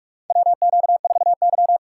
Morse code, heard as a keyed tone, about 46 wpm, 700 Hz. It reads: WQ4Y
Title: USA Call Signs - 4 Characters - 4x Character Spacing - 45wpm